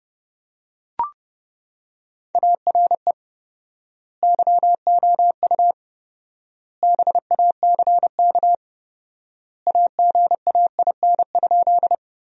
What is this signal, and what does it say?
Morse code, heard as a keyed tone, about 30 wpm, 700 Hz. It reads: ARE YOU BACK AGAIN?
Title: 4x Word-Spacing - Sentences from Top 200 Words - 30wpm